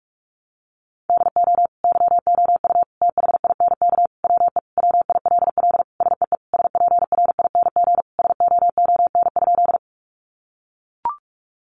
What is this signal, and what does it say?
Morse code, heard as a keyed tone, about 44 wpm, 700 Hz. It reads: DO YOU THINK WE WILL SEE SPRING SOON?
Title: Sentences from Top 1200 Words 45wpm